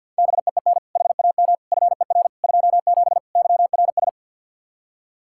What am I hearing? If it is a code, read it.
Morse code, heard as a keyed tone, about 52 wpm, 700 Hz. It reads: BEEN HAM FER 27 YRS
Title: K1USN SST QSO Practice 50wpm